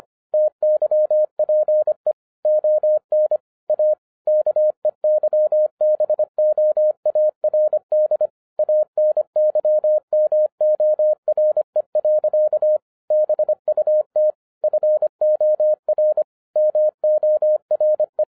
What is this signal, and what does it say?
Morse code, heard as a keyed tone, about 25 wpm, 610 Hz. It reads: TYPE ON A KEYBOARD ANYMORE. BUT FOR MORE
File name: 2011-10-mp3 / n09-CW TO PS2 CONVERTER 25wpm